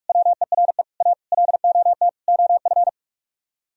Morse code, heard as a keyed tone, about 49 wpm, 700 Hz. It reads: WERE A LOT OF